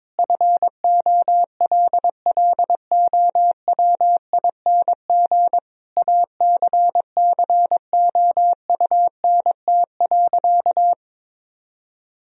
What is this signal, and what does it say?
Morse code, heard as a keyed tone, about 22 wpm, 700 Hz. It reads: FOLLOWING ACCOUNT.
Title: Sentences from Top 1000 Words - Review Part 4 - 22wpm